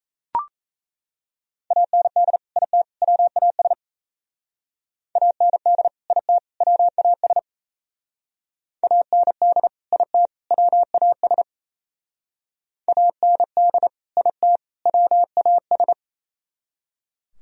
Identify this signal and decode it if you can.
Morse code, heard as a keyed tone, about 33 wpm, 700 Hz. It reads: AND IT WAS AND IT WAS AND IT WAS AND IT WAS
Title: Speed-Racing - Sets of 3 Words 28wpm